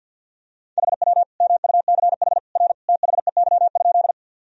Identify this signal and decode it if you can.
Morse code, heard as a keyed tone, about 49 wpm, 700 Hz. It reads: HW MUCH R THEY?
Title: Sentences from Top 200 Words and Abbreviations 50wpm